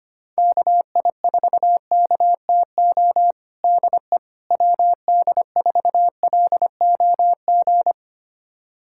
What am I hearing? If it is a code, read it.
Morse code, heard as a keyed tone, about 25 wpm, 700 Hz. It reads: KI4KTO DE WD4LOG
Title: USA Callsign DE Callsign - 6 Characters - 25wpm